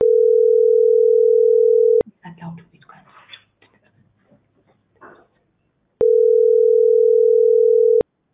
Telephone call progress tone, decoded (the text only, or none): ringback tone